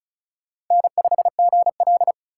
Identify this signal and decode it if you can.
Morse code, heard as a keyed tone, about 36 wpm, 700 Hz. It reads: N5GL
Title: Speed Racing - USA Call Signs 30wpm